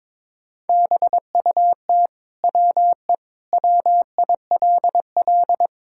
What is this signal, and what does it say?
Morse code, heard as a keyed tone, about 21 wpm, 700 Hz. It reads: BUT WE WILL
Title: Sentences from Top 1000 Words - Review Part 5 - 22wpm